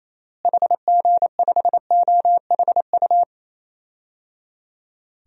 Morse code, heard as a keyed tone, about 28 wpm, 700 Hz. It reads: HG5OHU
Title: DXCC-100 Call Signs - 6 Characters - Repeated 3x - 28wpm